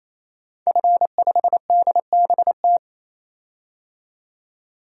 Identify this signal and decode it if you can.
Morse code, heard as a keyed tone, about 28 wpm, 700 Hz. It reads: F5DBT
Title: DXCC-100 Call Signs - 5 Characters - 28wpm